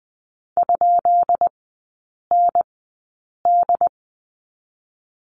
Morse code, heard as a keyed tone, about 20 wpm, spaced as for 8 wpm, 700 Hz. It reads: ?ND
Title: 133 Lesson 22e - Question Mark - Three Characters 20wpm